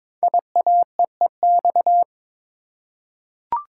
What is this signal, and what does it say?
Morse code, heard as a keyed tone, about 22 wpm, 700 Hz. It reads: IAEEX
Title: Code Groups - 5 Letter:Numbers - 22wpm